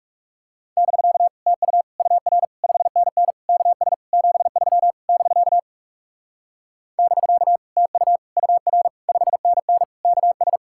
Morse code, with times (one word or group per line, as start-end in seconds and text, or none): 0.8-1.3 <BK>
1.5-1.8 TU
2.0-2.4 UR
2.6-3.3 5NN
3.5-3.9 KS
4.1-4.9 73
5.1-5.6 <BK>
7.0-7.6 <BK>
7.8-8.2 TU
8.4-8.9 UR
9.1-9.8 5NN
10.0-10.6 KS